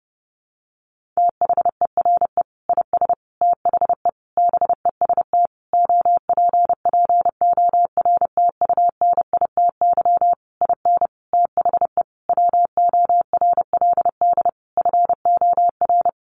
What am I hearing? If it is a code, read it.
Morse code, heard as a keyed tone, about 30 wpm, 700 Hz. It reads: THERE IS THE BEST OPPORTUNITY IN THE WORLD FOR
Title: Sentences from Top 1500 Words 30wpm